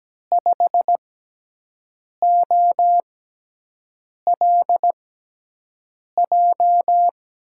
Morse code, heard as a keyed tone, about 17 wpm, spaced as for 6 wpm, 700 Hz. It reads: 5OLJ
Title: USA Call Signs - 6 Characters - 6x Character Spacing - 17wpm